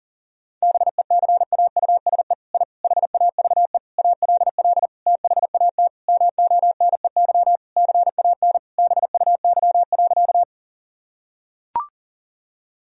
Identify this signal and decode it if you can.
Morse code, heard as a keyed tone, about 40 wpm, 700 Hz. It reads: BECAUSE I HAVE ALL THAT MONEY CAN BUY.
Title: Sentences from Top 1000 Words - Review Part 4 - 40wpm